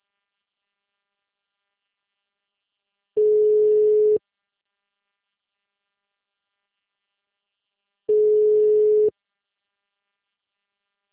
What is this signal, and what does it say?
Ringback tone (Europe, 425 Hz).
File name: out- -2017- - - .54147